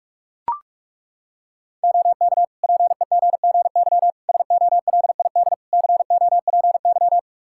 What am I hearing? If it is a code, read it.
Morse code, heard as a keyed tone, about 41 wpm, 700 Hz. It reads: OK PEGGY SOLID COPY
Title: Practice QSO - Exchange 3 - More Info - 45wpm